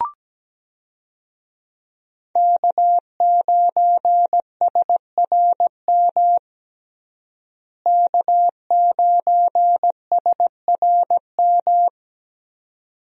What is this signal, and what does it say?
Morse code, heard as a keyed tone, about 17 wpm, 700 Hz. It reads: K9SRM K9SRM
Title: USA Call Signs - 5 Characters - Repeated 2x - 17wpm